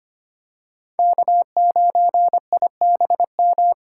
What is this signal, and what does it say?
Morse code, heard as a keyed tone, about 25 wpm, 700 Hz. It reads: K9IBM
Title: K1USN SST QSO Practice - 25wpm